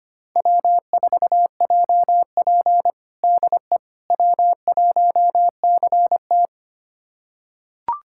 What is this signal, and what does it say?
Morse code, heard as a keyed tone, about 25 wpm, 700 Hz. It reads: W4JP DE W1CT
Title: USA Callsign DE Callsign - 4 Characters - 25wpm